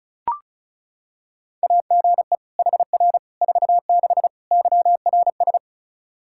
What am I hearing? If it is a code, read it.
Morse code, heard as a keyed tone, about 34 wpm, 700 Hz. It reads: AGE HR 46 YRS
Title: Practice QSO - Exchange 3 - More Info - 35wpm